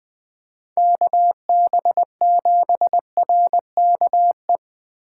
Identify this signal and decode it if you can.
Morse code, heard as a keyed tone, about 20 wpm, 700 Hz. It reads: KB7RKE
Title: USA Call Signs - 6 Characters - 3x Character Spacing - 20wpm